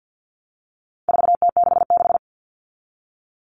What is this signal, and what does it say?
Morse code, heard as a keyed tone, about 48 wpm, 700 Hz. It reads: 4T66
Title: All Cut Numbers - 4-Digit 50wpm